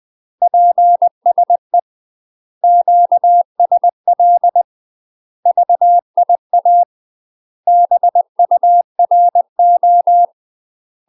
Morse code, heard as a keyed tone, about 20 wpm, 700 Hz. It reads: PSE QSL VIA BURO
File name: UK MASTER QSOs TRAINING / 20WPM / UK MASTER QSOs 0041